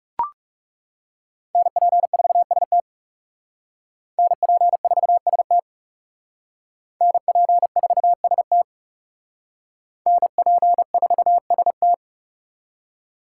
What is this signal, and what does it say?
Morse code, heard as a keyed tone, about 39 wpm, 700 Hz. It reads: NP4ST NP4ST NP4ST NP4ST